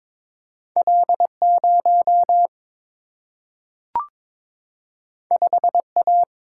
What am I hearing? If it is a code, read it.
Morse code, heard as a keyed tone, about 21 wpm, 700 Hz. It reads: L0 5A
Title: Single-Letter Single-Number 22wpm